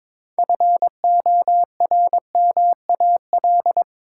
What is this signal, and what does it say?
Morse code, heard as a keyed tone, about 22 wpm, 700 Hz. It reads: FORMAL
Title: Sentences from Top 1300 Words - New Word Drill - 22wpm